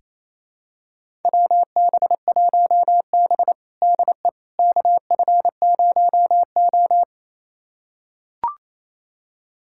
Morse code, heard as a keyed tone, about 28 wpm, 700 Hz. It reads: WB1B DE KF0O
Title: USA Callsign DE Callsign - 4 Characters - 28wpm